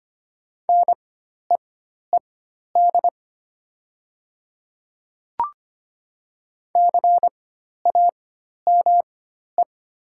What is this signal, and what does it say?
Morse code, heard as a keyed tone, about 25 wpm, spaced as for 12 wpm, 700 Hz. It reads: NEED CAME
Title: 4x Character Spacing - Top 300 Words - 25wpm